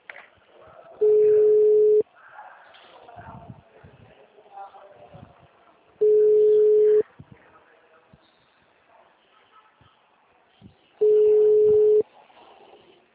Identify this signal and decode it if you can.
Ringback tone (Europe, 425 Hz), with speech.